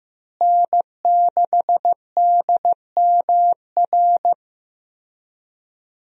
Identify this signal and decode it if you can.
Morse code, heard as a keyed tone, about 15 wpm, 700 Hz. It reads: N6DMR
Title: USA Call Signs - 5 Characters - Repeated 3x - 15wpm